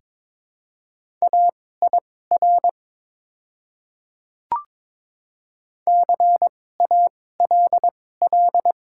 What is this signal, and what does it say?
Morse code, heard as a keyed tone, about 22 wpm, spaced as for 16 wpm, 700 Hz. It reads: AIR CALL
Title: 2x Character Spacing - Top 300 Words - 22wpm